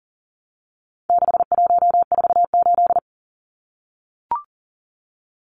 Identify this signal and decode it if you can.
Morse code, heard as a keyed tone, about 40 wpm, 700 Hz. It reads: 6148
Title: Numbers - 4-Digit 40wpm